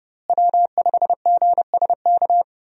Morse code, heard as a keyed tone, about 30 wpm, 700 Hz. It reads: W5GSK
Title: USA Call Signs - 5 Characters - Repeated 2x - 30wpm